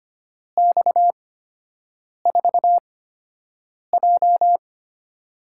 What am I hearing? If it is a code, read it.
Morse code, heard as a keyed tone, about 25 wpm, spaced as for 7 wpm, 700 Hz. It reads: X4J
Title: USA Call Signs - 4 Characters - 8x Character Spacing - 25wpm